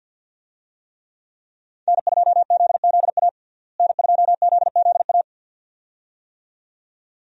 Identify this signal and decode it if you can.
Morse code, heard as a keyed tone, about 49 wpm, 700 Hz. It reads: N2ZZA N2ZZA
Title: USA Call Signs - 5 Characters - Repeated 2x - 50wpm